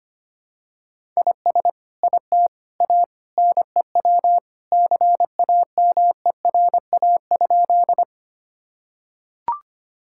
Morse code, heard as a keyed tone, about 25 wpm, 700 Hz. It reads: IS IT A NEW CAMERA?
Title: Sentences from Top 1000 Words - Review Part 5 - 25wpm